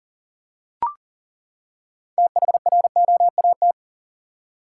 Morse code, heard as a keyed tone, about 40 wpm, 700 Hz. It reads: THROAT
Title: Six Letter Words - 40wpm